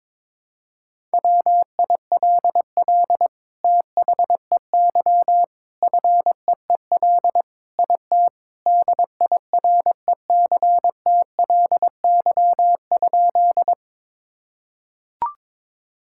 Morse code, heard as a keyed tone, about 22 wpm, 700 Hz. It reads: WILL THEY FEEL IT DIRECTLY?
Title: Sentences from Top 1000 Words - Review Part 5 - 22wpm